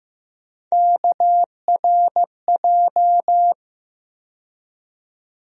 Morse code, heard as a keyed tone, about 15 wpm, 700 Hz. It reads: KRJ
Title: USA Call Sign Postfixes - 3 Letters 15wpm